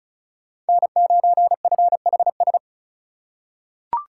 Morse code, heard as a keyed tone, about 36 wpm, 700 Hz. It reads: N9FHS
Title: USA Call Signs - 5 Characters - 4x Character Spacing - 35wpm